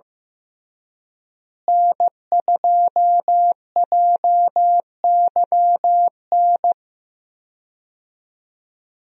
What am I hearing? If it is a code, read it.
Morse code, heard as a keyed tone, about 15 wpm, 700 Hz. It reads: N2JYN
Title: USA Call Signs - 5 Characters - Repeated 3x - 15wpm